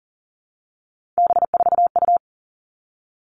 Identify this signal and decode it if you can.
Morse code, heard as a keyed tone, about 40 wpm, 700 Hz. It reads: B4U
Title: All Cut Numbers - 3-Digit 40wpm